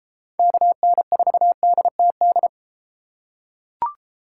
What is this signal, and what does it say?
Morse code, heard as a keyed tone, about 31 wpm, 700 Hz. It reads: KN4DTD